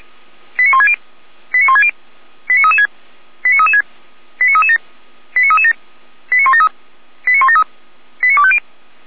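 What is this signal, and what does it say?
Five-tone selective call (ZVEI), heard as quick runs of five tones, tones of about 70 ms. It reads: 89160 89160 89336 89336 89337 89337 89173 89173 89250
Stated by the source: ZVEI five-tone selective call received on an SDR, archived by fire brigade Orlamünde (feuerwehr-orlamuende).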